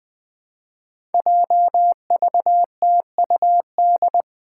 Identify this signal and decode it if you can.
Morse code, heard as a keyed tone, about 20 wpm, 700 Hz. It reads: JVTUD
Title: Five Challenging Letter-Number Combinations 20wpm